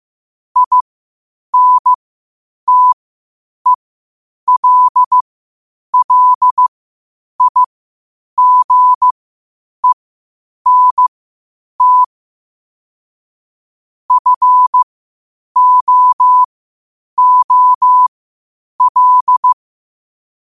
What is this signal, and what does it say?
Morse code, heard as a keyed tone, about 15 wpm, spaced as for 8 wpm, 1000 Hz. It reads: INTELLIGENT FOOL
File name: qotdfiles / 8wpm